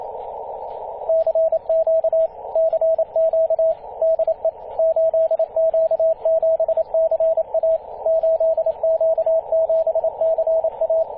Morse code, heard as a keyed tone, about 28 wpm, 650 Hz, with radio noise. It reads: CQ CQ DE 8Q7CA 8Q7CA
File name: 8q7ca-21-013mhz-cw